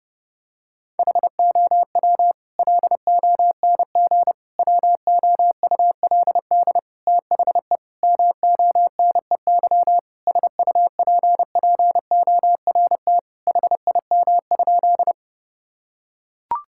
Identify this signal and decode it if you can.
Morse code, heard as a keyed tone, about 30 wpm, 700 Hz. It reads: HOW LONG WOULD THE MONEY SUPPORT HIM?
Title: Sentences from Top 1000 Words - Review Part 5 - 30wpm